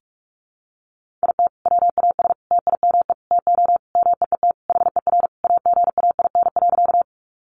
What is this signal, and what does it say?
Morse code, heard as a keyed tone, about 44 wpm, 700 Hz. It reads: IT WAS TIME TO MEET HER AGAIN.
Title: Sentences from Top 1100 Words 45wpm